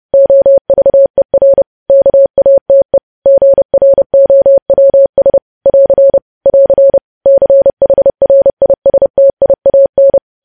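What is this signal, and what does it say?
Morse code, heard as a keyed tone, about 30 wpm, 550 Hz. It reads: OVER KATE GROWS <AR> <AR> CHRISTIAN
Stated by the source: Morse code sending of news headlines at 30 wpm.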